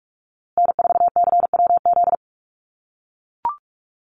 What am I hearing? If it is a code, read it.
Morse code, heard as a keyed tone, about 44 wpm, 700 Hz. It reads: N4CWZ
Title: USA Call Signs Repeated 3x 45wpm